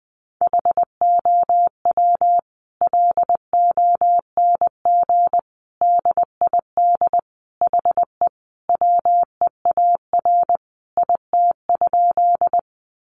Morse code, heard as a keyed tone, about 20 wpm, 700 Hz. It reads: HOW LONG DID HE WEAR IT?
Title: Sentences from Top 1500 Words 20wpm